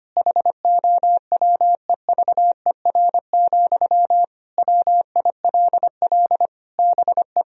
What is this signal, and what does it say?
Morse code, heard as a keyed tone, about 25 wpm, 700 Hz. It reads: HOWEVER, WILL BE
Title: Sentences from Top 1000 Words - Review Part 4 - 25wpm